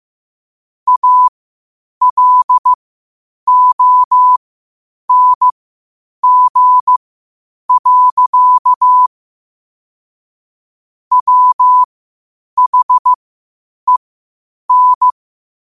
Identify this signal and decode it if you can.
Morse code, heard as a keyed tone, about 15 wpm, spaced as for 8 wpm, 1000 Hz. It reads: ALONG. WHEN
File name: qotdfiles / 8wpm